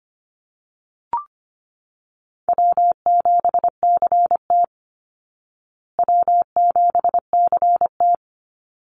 Morse code, heard as a keyed tone, about 25 wpm, 700 Hz. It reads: W7CT W7CT
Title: Easy Call Signs Repeated 3x - 25wpm